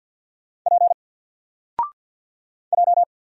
Morse code, heard as a keyed tone, about 50 wpm, 700 Hz. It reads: P J
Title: Single Letter-Number 50wpm